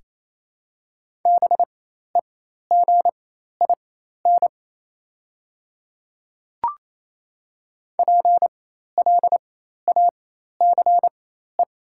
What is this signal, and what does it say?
Morse code, heard as a keyed tone, about 28 wpm, spaced as for 13 wpm, 700 Hz. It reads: BEGIN PLACE
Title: 4x Character Spacing - Top 500 Words - 28wpm